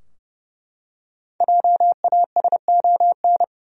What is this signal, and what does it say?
Morse code, heard as a keyed tone, about 30 wpm, 700 Hz. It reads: JASON
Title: USA Common Names - Max 5 Letters - 30wpm